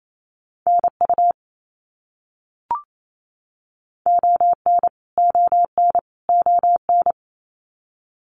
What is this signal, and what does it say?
Morse code, heard as a keyed tone, about 28 wpm, 700 Hz. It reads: NU ON ON ON
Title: Canadian Provinces and Territories - Repeated 3x - 28wpm